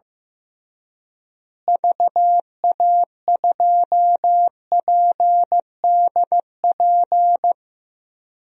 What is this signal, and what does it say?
Morse code, heard as a keyed tone, about 15 wpm, 700 Hz. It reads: VA2PDP
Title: DXCC-100 Call Signs - 6 Characters - Repeated 2x - 15wpm